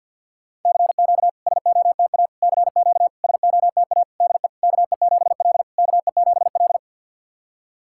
Morse code, heard as a keyed tone, about 50 wpm, 700 Hz. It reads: CQ SOTA CQ SOTA DE KE7L KE7L
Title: SOTA QSO Practice 50wpm